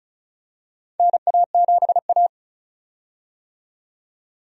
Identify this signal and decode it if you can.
Morse code, heard as a keyed tone, about 34 wpm, 700 Hz. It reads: NA7A
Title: USA Call Signs - 4 Characters - 35wpm